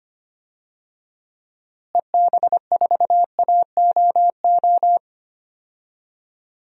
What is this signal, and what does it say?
Morse code, heard as a keyed tone, about 25 wpm, 700 Hz. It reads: EB4AOO
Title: DXCC-100 Call Signs - 6 Characters - Repeated 3x - 25wpm